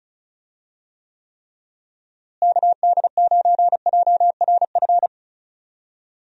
Morse code, heard as a keyed tone, about 34 wpm, 700 Hz. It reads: KD9JRF
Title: USA Call Signs - 6 Characters - Repeated 3x - 35wpm